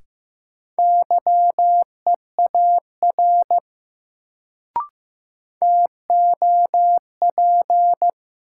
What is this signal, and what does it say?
Morse code, heard as a keyed tone, about 15 wpm, 700 Hz. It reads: YEAR TOP
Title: Sets of 2 Words - New Word Drill - 15wpm